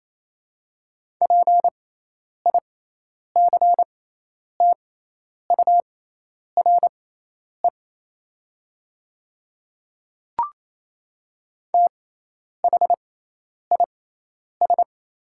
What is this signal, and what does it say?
Morse code, heard as a keyed tone, about 28 wpm, spaced as for 10 wpm, 700 Hz. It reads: PICTURE THIS
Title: 6x Character Spacing - Top 200 Words - 28wpm